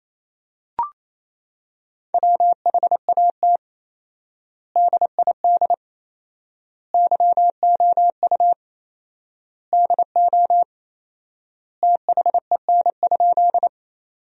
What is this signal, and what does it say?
Morse code, heard as a keyed tone, about 27 wpm, 700 Hz. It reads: WHAT DID YOU DO THEN?
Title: 4x Word-Spacing - Sentences from Top 100 Words - 28wpm